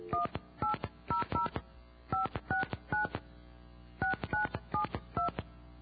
DTMF, heard as telephone touch tones, tones of about 120 ms. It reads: 1800 569 6972